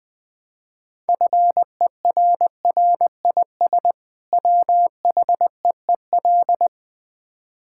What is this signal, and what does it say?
Morse code, heard as a keyed tone, about 20 wpm, 700 Hz. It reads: FERRIS WHEEL
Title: Easy Words with Hints - 20wpm